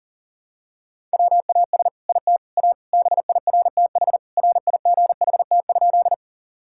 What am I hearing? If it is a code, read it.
Morse code, heard as a keyed tone, about 40 wpm, 700 Hz. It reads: WAS IT A BIRTH RIGHT?
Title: Sentences from Top 1000 Words - Review Part 5 - 40wpm